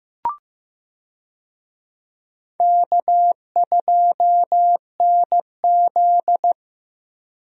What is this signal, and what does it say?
Morse code, heard as a keyed tone, about 15 wpm, 700 Hz. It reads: K2NZ
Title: USA Call Signs - 4 Characters - Repeated 3x - 15wpm